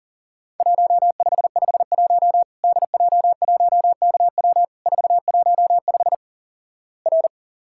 Morse code, heard as a keyed tone, about 43 wpm, 700 Hz. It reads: 1551 DJ1KW 415 R
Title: Worked All Europe DX Contest - QTC Exchange - 40wpm